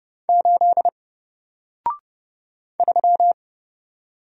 Morse code, heard as a keyed tone, about 30 wpm, 700 Hz. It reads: 8 3
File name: Single-Letter-Number-30wpm